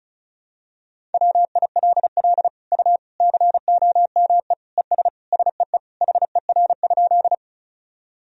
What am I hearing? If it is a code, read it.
Morse code, heard as a keyed tone, about 34 wpm, 700 Hz. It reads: WILL U COME ES SEE HER?
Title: Sentences from Top 100 Words and Abbreviations 35wpm